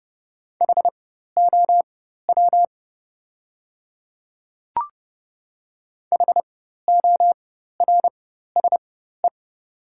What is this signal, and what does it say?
Morse code, heard as a keyed tone, about 30 wpm, spaced as for 14 wpm, 700 Hz. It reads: HOW HORSE
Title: 4x Character Spacing - Top 300 Words - 30wpm